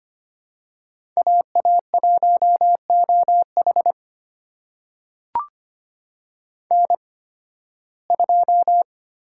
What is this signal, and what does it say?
Morse code, heard as a keyed tone, about 25 wpm, spaced as for 16 wpm, 700 Hz. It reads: AA1OH N2
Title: USA Call Signs - 5 Characters - 8x Character Spacing - 25wpm